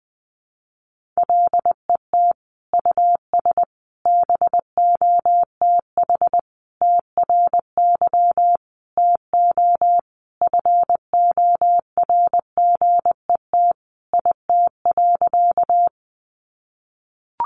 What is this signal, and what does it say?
Morse code, heard as a keyed tone, about 20 wpm, 700 Hz. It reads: LET US BOTH TRY TO FORGET IT.
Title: Sentences from Top 1200 Words 20wpm